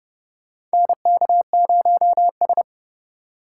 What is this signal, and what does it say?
Morse code, heard as a keyed tone, about 30 wpm, 700 Hz. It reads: NK0S
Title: USA Call Signs - 4 Characters - 2x Character Spacing - 30wpm